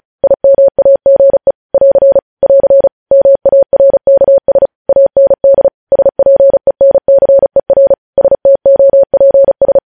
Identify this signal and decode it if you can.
Morse code, heard as a keyed tone, about 36 wpm, 550 Hz. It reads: IMAGE <AR> <AR> MARKS AND SPENCER STOPS
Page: News headlines in Morse code at 35 wpm